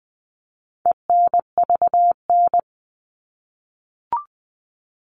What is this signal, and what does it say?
Morse code, heard as a keyed tone, about 20 wpm, 700 Hz. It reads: ENVN
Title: All Cut Numbers - 4-Digit 20wpm